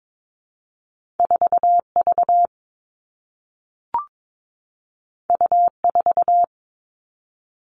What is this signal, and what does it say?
Morse code, heard as a keyed tone, about 22 wpm, 700 Hz. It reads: 4 V U 4
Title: Two Challenging Letter-Number Combinations 22wpm